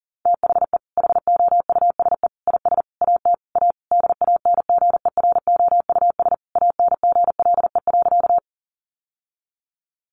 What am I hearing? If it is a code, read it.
Morse code, heard as a keyed tone, about 40 wpm, 700 Hz. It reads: THE HOUSE IS AT A DANGEROUS ANGLE.
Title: Sentences from Top 1500 Words 40wpm